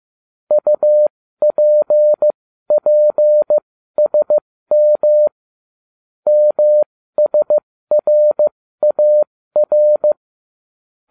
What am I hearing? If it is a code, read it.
Morse code, heard as a keyed tone, about 15 wpm, spaced as for 13 wpm, 600 Hz. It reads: UPPSM MSRAR